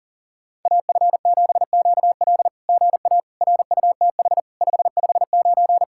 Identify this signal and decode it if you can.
Morse code, heard as a keyed tone, about 40 wpm, 700 Hz. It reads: AF7QL GA RUTH 559
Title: SOTA QSO Practice 40wpm